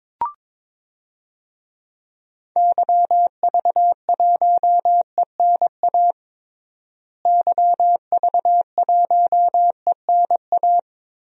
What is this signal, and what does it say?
Morse code, heard as a keyed tone, about 22 wpm, 700 Hz. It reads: YV1ENA YV1ENA
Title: DXCC-100 Call Signs - 6 Characters - Repeated 2x - 22wpm